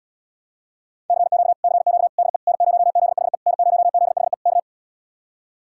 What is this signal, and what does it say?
Morse code, heard as a keyed tone, about 80 wpm, 700 Hz. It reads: CQ CQ DE N0ZHE N0ZHE K